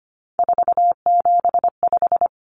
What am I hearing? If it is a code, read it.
Morse code, heard as a keyed tone, about 25 wpm, 700 Hz. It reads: 475
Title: Common Cut Numbers - 3-Digit 25wpm